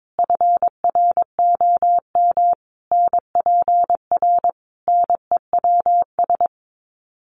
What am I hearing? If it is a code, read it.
Morse code, heard as a keyed tone, about 22 wpm, 700 Hz. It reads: FROM NPR NEWS
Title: Sets of 5 Words Encore 22wpm